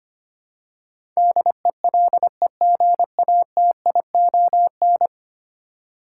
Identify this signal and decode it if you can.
Morse code, heard as a keyed tone, about 25 wpm, 700 Hz. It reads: DELEGATION
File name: Ten Letter Words - 25wpm